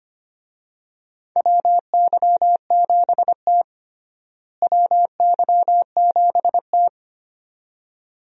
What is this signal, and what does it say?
Morse code, heard as a keyed tone, about 25 wpm, 700 Hz. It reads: WY7T WY7T
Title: USA Call Signs - 4 Characters - Repeated 2x - 25wpm